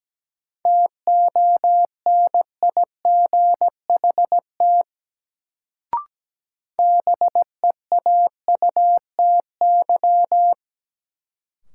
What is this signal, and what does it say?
Morse code, heard as a keyed tone, about 17 wpm, 700 Hz. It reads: TONIGHT BEAUTY
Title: Sentences from Top 600 Words - New Word Drill - 17wpm